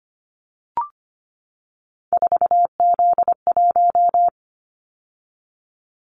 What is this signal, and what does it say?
Morse code, heard as a keyed tone, about 25 wpm, 690 Hz. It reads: 4Z1
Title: Callsign Prefixes for Top 50 DXCC - 25wpm